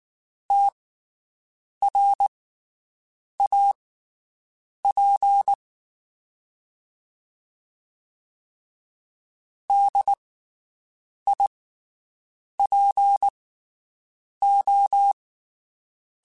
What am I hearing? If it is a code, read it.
Morse code, heard as a keyed tone, about 19 wpm, spaced as for 7 wpm, 800 Hz. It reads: TRAP DIPO
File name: QSOs Spaced